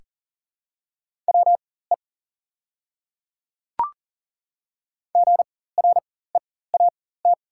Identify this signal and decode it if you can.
Morse code, heard as a keyed tone, about 40 wpm, spaced as for 19 wpm, 690 Hz. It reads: WE GREAT
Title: 4x Character Spacing - Top 200 Words - 40wpm